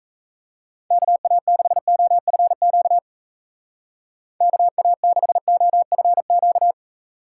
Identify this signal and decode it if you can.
Morse code, heard as a keyed tone, about 41 wpm, 700 Hz. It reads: KA6OFQ KA6OFQ